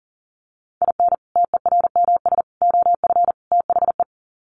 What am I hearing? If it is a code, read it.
Morse code, heard as a keyed tone, about 38 wpm, 700 Hz. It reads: IN TERMS OF THE
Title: Sets of 4 Words 40wpm